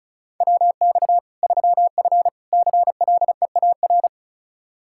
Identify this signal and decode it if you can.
Morse code, heard as a keyed tone, about 34 wpm, 700 Hz. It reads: WX 3F CLEAR
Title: K1USN SST QSO Practice 35wpm